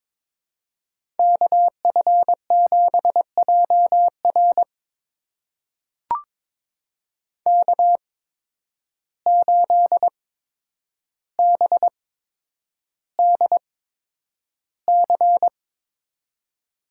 Morse code, heard as a keyed tone, about 22 wpm, spaced as for 9 wpm, 700 Hz. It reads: KF7JR K8BDC
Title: USA Call Signs - 5 Characters - 8x Character Spacing - 22wpm